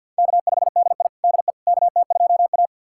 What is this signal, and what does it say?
Morse code, heard as a keyed tone, about 51 wpm, 700 Hz. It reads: K5DI DE KT2A
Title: USA Callsign DE Callsign - 4 Characters - 50wpm